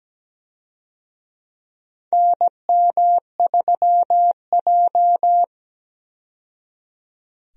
Morse code, heard as a keyed tone, about 17 wpm, 700 Hz. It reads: NM3J
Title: USA Call Signs - 4 Characters - 17wpm